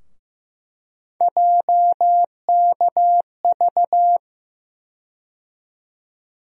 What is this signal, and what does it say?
Morse code, heard as a keyed tone, about 15 wpm, 700 Hz. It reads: JKV
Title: USA Call Sign Postfixes - 3 Letters 15wpm